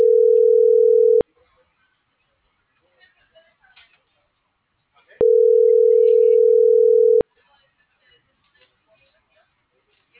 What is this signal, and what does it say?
Ringback tone (North America), 440 + 480 Hz.